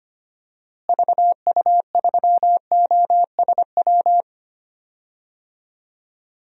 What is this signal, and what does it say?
Morse code, heard as a keyed tone, about 25 wpm, 700 Hz. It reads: VU3OSW
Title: DXCC-100 Call Signs - 6 Characters - Repeated 2x - 25wpm